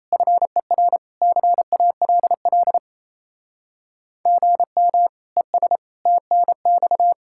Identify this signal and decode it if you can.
Morse code, heard as a keyed tone, about 33 wpm, 700 Hz. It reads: FER CALL GM ES TNX
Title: Speed-Racing Common QSO Phrases - 28wpm